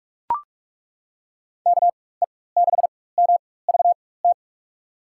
Morse code, heard as a keyed tone, about 42 wpm, spaced as for 21 wpm, 700 Hz. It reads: KE6MVT
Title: USA Call Signs - 6 Characters - 4x Character Spacing - 45wpm